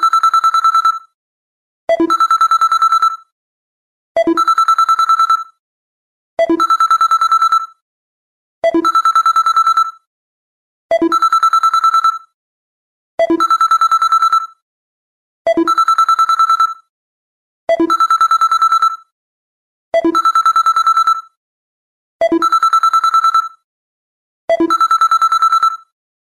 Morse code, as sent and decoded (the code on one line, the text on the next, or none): none